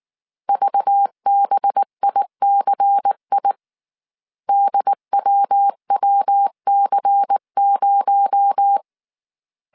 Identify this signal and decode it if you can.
Morse code, heard as a keyed tone, about 19 wpm, 800 Hz. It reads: VBICI DWWC0
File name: CW / DET / 1 6 1500C S C 18wpm 2356